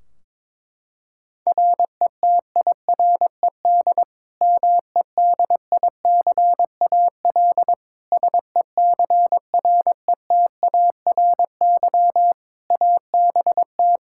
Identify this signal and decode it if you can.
Morse code, heard as a keyed tone, about 22 wpm, 700 Hz. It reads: RETIRED MEDICAL SECRETARY ABT
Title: Practice QSO - Exchange 3 - More Info - 22wpm